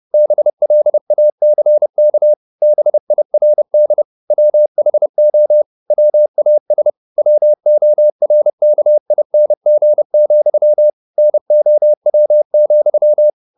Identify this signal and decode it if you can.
Morse code, heard as a keyed tone, about 30 wpm, 600 Hz. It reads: BLACK BIRD WHO WAS WORKING, NOW,
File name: the old man and the sea / the old man and the sea 1 30wpm